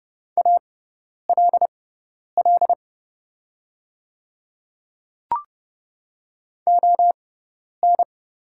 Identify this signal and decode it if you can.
Morse code, heard as a keyed tone, about 29 wpm, spaced as for 10 wpm, 700 Hz. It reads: ALL ON
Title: 6x Character Spacing - Top 100 Words - 30wpm